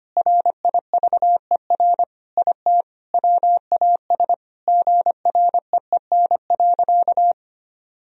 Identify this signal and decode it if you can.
Morse code, heard as a keyed tone, about 25 wpm, 700 Hz. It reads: RIVER IT WAS GREEN.
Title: Sentences from Top 1000 Words - Review Part 2 - 25wpm